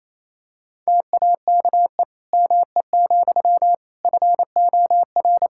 Morse code, heard as a keyed tone, about 28 wpm, 700 Hz. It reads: TAKE ME, FOR
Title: Sentences from Top 1000 Words - Review Part 5 - 28wpm